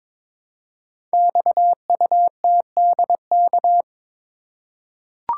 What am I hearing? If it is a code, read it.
Morse code, heard as a keyed tone, about 22 wpm, 700 Hz. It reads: XUTDK
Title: Code Groups - 5 Letters - 22wpm